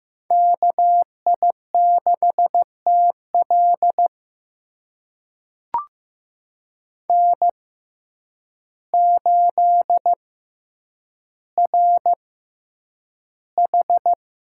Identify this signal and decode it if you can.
Morse code, heard as a keyed tone, about 15 wpm, spaced as for 8 wpm, 700 Hz. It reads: KI6TL N8RH
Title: USA Call Signs - 5 Characters - 6x Character Spacing - 15wpm